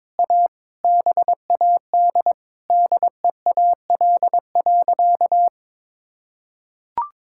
Morse code, heard as a keyed tone, about 22 wpm, 700 Hz. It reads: A BAD DEAL.
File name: practice / 22wpm / Sentences with 8-Word Phrases - 22wpm